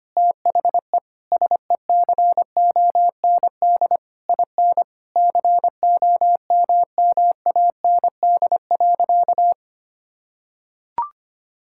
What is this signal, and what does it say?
Morse code, heard as a keyed tone, about 25 wpm, 700 Hz. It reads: THE SECOND IN COMMAND.
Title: Sentences from Top 1600 Words 25wpm